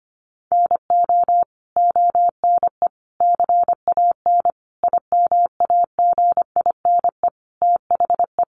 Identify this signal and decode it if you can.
Morse code, heard as a keyed tone, about 25 wpm, 700 Hz. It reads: NO ONE CAN IMAGINE THE
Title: Sentences from Top 1400 Words 25wpm